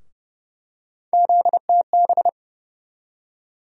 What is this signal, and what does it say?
Morse code, heard as a keyed tone, about 30 wpm, 700 Hz. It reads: ZTB
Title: USA Call Sign Postfixes - 3 Letters 30wpm